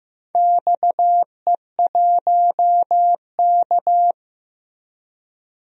Morse code, heard as a keyed tone, about 15 wpm, 700 Hz. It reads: XE1K
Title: DXCC-100 Call Signs - 4 Characters - Repeated 2x - 15wpm